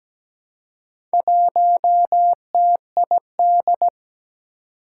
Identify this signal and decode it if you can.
Morse code, heard as a keyed tone, about 17 wpm, 700 Hz. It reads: 1TID
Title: Four Challenging Letter-Number Combinations 17wpm